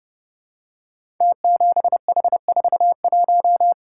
Morse code, heard as a keyed tone, about 28 wpm, 700 Hz. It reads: T7H41
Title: Code Groups - 5 Letter:Numbers - 30wpm